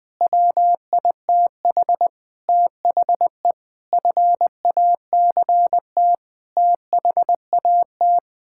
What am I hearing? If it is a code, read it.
Morse code, heard as a keyed tone, about 20 wpm, 700 Hz. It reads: WITH THE FACT THAT
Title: Sets of 8 Words - 20wpm